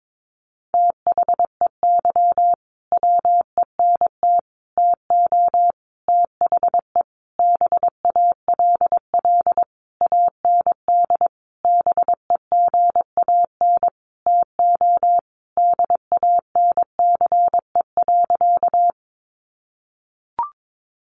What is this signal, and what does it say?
Morse code, heard as a keyed tone, about 22 wpm, 700 Hz. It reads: THEY WENT TO THE BALL AND BEGAN TO DANCE.
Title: Sentences from Top 900 Words 22wpm